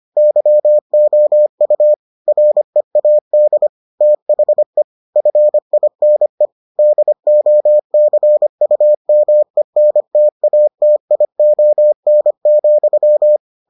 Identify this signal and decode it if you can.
Morse code, heard as a keyed tone, about 25 wpm, 600 Hz. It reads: YOU READ THE FINE DOCUMENTATION,